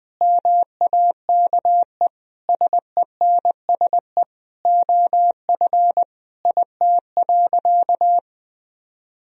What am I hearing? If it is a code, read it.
Morse code, heard as a keyed tone, about 21 wpm, 700 Hz. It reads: MAKE SENSE OF IT.
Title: Sentences from Top 1000 Words - Review Part 4 - 20wpm